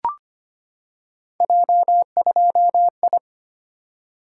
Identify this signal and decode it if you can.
Morse code, heard as a keyed tone, about 25 wpm, 700 Hz. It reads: J2I
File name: Three Letter-Number Combinations - 25wpm